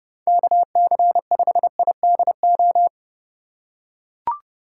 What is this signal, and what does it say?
Morse code, heard as a keyed tone, about 30 wpm, 700 Hz. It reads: KC5IDO
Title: USA Call Signs - 6 Characters - 4x Character Spacing - 30wpm